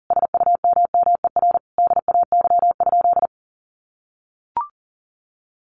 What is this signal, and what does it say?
Morse code, heard as a keyed tone, about 40 wpm, 700 Hz. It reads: SUMMER DAY?
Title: Sentences from Top 1500 Words 40wpm